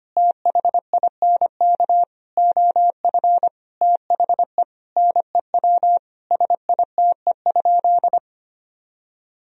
Morse code, heard as a keyed tone, about 25 wpm, 700 Hz. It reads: THINK OF THE NEW SITE?
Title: Sentences from Top 1000 Words - Review Part 4 - 25wpm